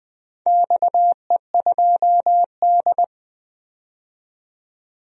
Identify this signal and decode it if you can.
Morse code, heard as a keyed tone, about 20 wpm, 700 Hz. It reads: XE2D
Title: DXCC-100 Call Signs - 4 Characters - 20wpm